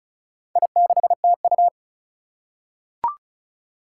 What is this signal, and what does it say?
Morse code, heard as a keyed tone, about 36 wpm, 700 Hz. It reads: I6TU
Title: Four Challenging Letter-Number Combinations 35wpm